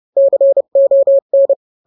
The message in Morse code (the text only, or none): CON